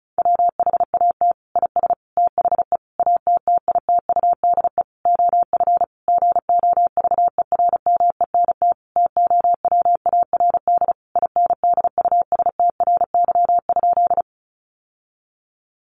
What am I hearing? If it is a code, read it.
Morse code, heard as a keyed tone, about 34 wpm, 700 Hz. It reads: WHAT IS THE ATTITUDE OF GOVERMENT TOWARD INDUSTRY?
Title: Sentences from Top 1300 Words 35wpm